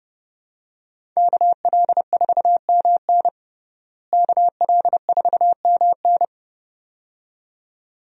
Morse code, heard as a keyed tone, about 32 wpm, 700 Hz. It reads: KL4MN KL4MN
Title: USA Call Signs - 5 Characters - Repeated 2x - 30wpm